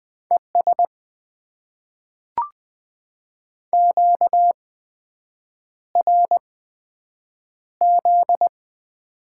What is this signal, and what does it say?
Morse code, heard as a keyed tone, about 20 wpm, spaced as for 7 wpm, 700 Hz. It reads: ES QRZ
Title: Top 90 QSO Elements - 8x Character Spacing - 20wpm